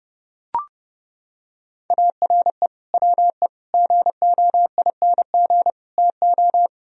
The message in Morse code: ARE WE GOING TO

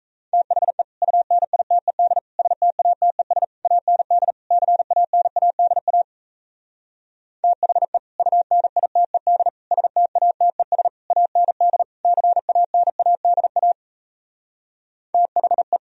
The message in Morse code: THE UNITED STATES AND CANADA THE UNITED STATES AND CANADA THE

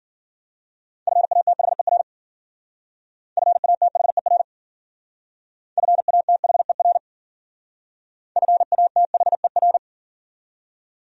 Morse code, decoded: FATHER FATHER FATHER FATHER

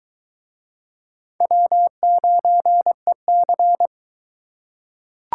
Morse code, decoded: W9EC

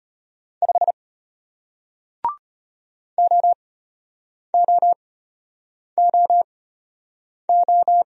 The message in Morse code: 5 OOOO

